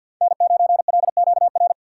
Morse code, heard as a keyed tone, about 47 wpm, 700 Hz. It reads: N9LQR
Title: USA Call Signs - 3x Character Spacing - 50wpm